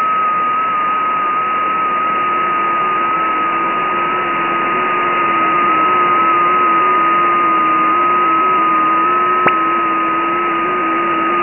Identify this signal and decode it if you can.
FT8.